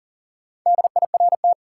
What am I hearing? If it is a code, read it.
Morse code, heard as a keyed tone, about 40 wpm, 700 Hz. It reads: DIRT
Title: Four Letter Words - 40wpm